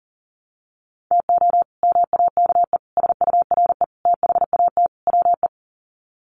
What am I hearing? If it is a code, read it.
Morse code, heard as a keyed tone, about 37 wpm, 700 Hz. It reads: TO MAKE SURE THAT WE
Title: Sets of 5 Words Encore 40wpm